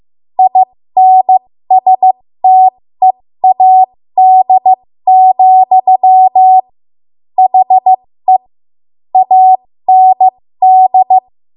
Morse code, heard as a keyed tone, about 15 wpm, spaced as for 13 wpm, 750 Hz. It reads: INSTEAD, HE AND
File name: Morse / 13WPM